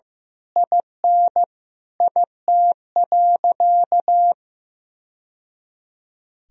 Morse code, heard as a keyed tone, about 16 wpm, 700 Hz. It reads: IN IT.
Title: Sentences from Top 1000 Words - Review Part 3 - 15wpm